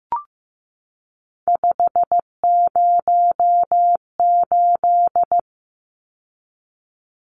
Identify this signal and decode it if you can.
Morse code, heard as a keyed tone, about 15 wpm, 700 Hz. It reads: 508
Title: Numbers - 3-Digit 15wpm